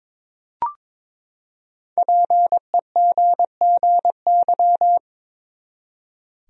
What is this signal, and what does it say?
Morse code, heard as a keyed tone, about 22 wpm, 700 Hz. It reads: PEGGY
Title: USA Common Names - Max 5 Letters - 22wpm